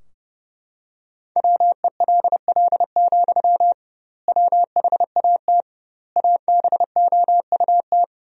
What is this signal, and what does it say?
Morse code, heard as a keyed tone, about 30 wpm, 700 Hz. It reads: WELL, WHAT ABOUT
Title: 2x Word-Spacing - Sentences from Top 500 Words - 30wpm